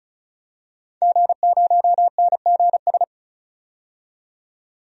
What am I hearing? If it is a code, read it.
Morse code, heard as a keyed tone, about 36 wpm, 700 Hz. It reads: G0NGS
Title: DXCC-100 Call Signs - 5 Characters - 35wpm